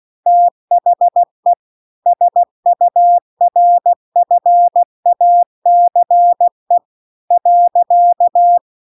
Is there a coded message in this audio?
Yes: Morse code from 0.3 to 8.6 s, heard as a keyed tone, about 16 wpm, 700 Hz. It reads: THE SURFACE .